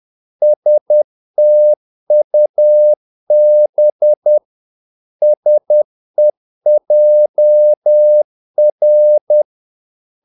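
Morse code, heard as a keyed tone, about 10 wpm, 600 Hz. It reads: STUB SEJR
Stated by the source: Morse practice of short Danish words at 10 wpm, from CW med Gnister.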